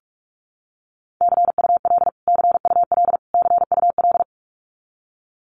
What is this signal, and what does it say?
Morse code, heard as a keyed tone, about 44 wpm, 700 Hz. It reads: CUL CUL CUL
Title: Abbreviations Repeated 3x - 45wpm